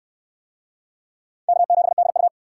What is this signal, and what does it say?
Morse code, heard as a keyed tone, about 68 wpm, 700 Hz. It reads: K7DU